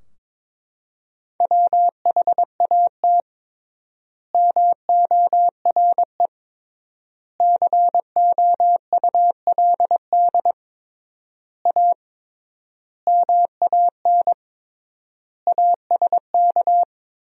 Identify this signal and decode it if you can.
Morse code, heard as a keyed tone, about 22 wpm, 700 Hz. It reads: WHAT MORE COULD A MAN ASK
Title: 3x Word-Spacing - Sentences from Top 200 Words - 22wpm